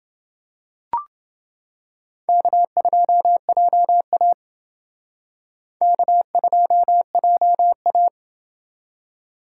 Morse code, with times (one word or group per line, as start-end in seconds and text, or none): 2.3-4.3 K2JA
5.8-8.1 K2JA